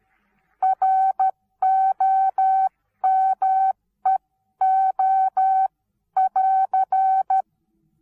Morse code, heard as a keyed tone, about 13 wpm, spaced as for 11 wpm, 760 Hz. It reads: ROMEO<AR>
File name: Onlinelehrgang / morsen / DJ4UF Morsen19